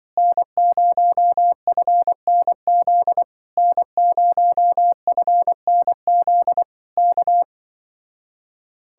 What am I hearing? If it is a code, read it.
Morse code, heard as a keyed tone, about 23 wpm, 700 Hz. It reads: N0FNZ N0FNZ K
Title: Speed-Racing Common QSO Phrases - 20wpm